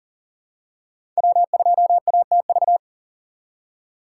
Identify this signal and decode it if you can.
Morse code, heard as a keyed tone, about 40 wpm, 700 Hz. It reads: W2ATV
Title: USA Call Signs - 5 Characters - 4x Character Spacing - 40wpm